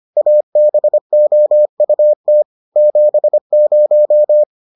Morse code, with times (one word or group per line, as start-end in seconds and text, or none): 0.2-2.4 ABOUT
2.8-4.4 70